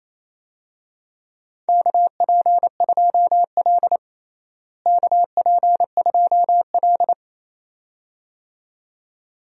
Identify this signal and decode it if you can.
Morse code, heard as a keyed tone, about 28 wpm, 700 Hz. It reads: KP2L KP2L
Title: DXCC-100 Call Signs - 4 Characters - Repeated 2x - 28wpm